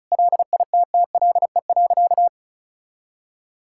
Morse code, heard as a keyed tone, about 36 wpm, 700 Hz. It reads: LITTLE.
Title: Sentences from Top 1600 Words 35wpm